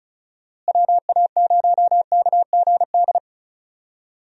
Morse code, heard as a keyed tone, about 36 wpm, 700 Hz. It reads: WA0KGD
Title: USA Call Signs - 6 Characters - Repeated 2x - 35wpm